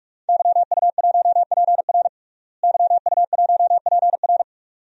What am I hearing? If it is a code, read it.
Morse code, heard as a keyed tone, about 45 wpm, 700 Hz. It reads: YU1PR YU1PR
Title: DXCC-100 Call Signs - 5 Characters - Repeated 2x - 45wpm